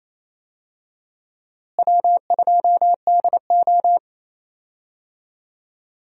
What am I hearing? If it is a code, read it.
Morse code, heard as a keyed tone, about 28 wpm, 700 Hz. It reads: W2DO
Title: USA Call Signs - 4 Characters - Repeated 3x - 28wpm